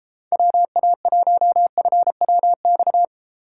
Morse code, heard as a keyed tone, about 33 wpm, 700 Hz. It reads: WA1FWX